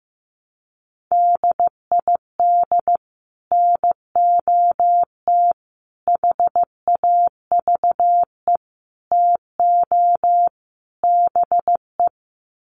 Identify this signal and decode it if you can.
Morse code, heard as a keyed tone, about 15 wpm, 700 Hz. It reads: DID NOT HAVE TO BE
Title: Sets of 5 Words Encore2 15wpm